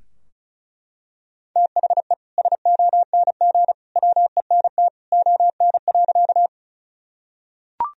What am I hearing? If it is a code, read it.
Morse code, heard as a keyed tone, about 36 wpm, 700 Hz. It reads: THE SONG WENT ON.
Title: Sentences from Top 1000 Words - Review Part 2 - 35wpm